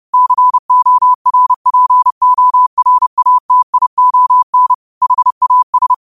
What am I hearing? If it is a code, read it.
Morse code, heard as a keyed tone, about 30 wpm, 1000 Hz. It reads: CORPORATION HAS
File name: qotdfiles / 30wpm